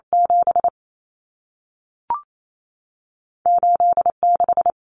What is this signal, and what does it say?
Morse code, heard as a keyed tone, about 28 wpm, spaced as for 3 wpm, 700 Hz. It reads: 78 6
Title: Common Cut Numbers - 2-Digit 28wpm